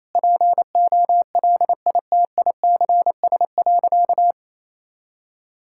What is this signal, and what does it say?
Morse code, heard as a keyed tone, about 28 wpm, 700 Hz. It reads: POLITICS.
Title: Sentences from Top 1000 Words - Review Part 6 - 28wpm